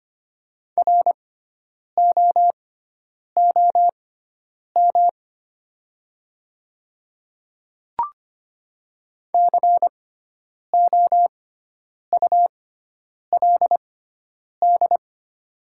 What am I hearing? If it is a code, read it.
Morse code, heard as a keyed tone, about 25 wpm, spaced as for 9 wpm, 700 Hz. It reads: ROOM COULD
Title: 6x Character Spacing - Top 300 Words - 25wpm